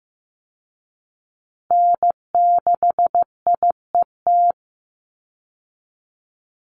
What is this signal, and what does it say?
Morse code, heard as a keyed tone, about 15 wpm, 700 Hz. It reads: N6IET
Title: Call Signs Repeated 2x - 15wpm